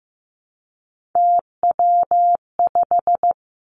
Morse code, heard as a keyed tone, about 15 wpm, 700 Hz. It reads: TW5
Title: Three Challenging Letter-Number Combinations 15wpm